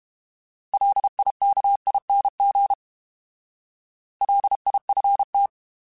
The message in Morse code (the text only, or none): LIKING LIFT